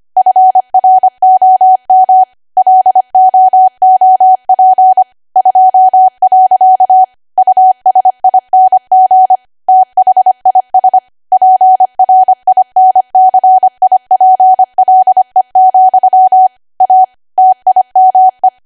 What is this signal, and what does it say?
Morse code, heard as a keyed tone, about 25 wpm, 750 Hz. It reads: FROM LOOP 2. USING THIS PRINCIPLE, A TIME
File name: Archive / 25 WPM / 25WPM